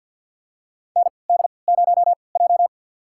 Morse code, heard as a keyed tone, about 51 wpm, spaced as for 28 wpm, 700 Hz. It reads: ND0J
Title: USA Call Signs - 4 Characters - 3x Character Spacing - 50wpm